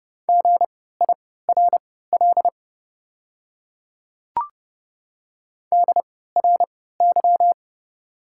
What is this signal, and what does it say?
Morse code, heard as a keyed tone, about 30 wpm, spaced as for 17 wpm, 700 Hz. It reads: GIRL DRY
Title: 3x Character Spacing - Top 500 Words - 30wpm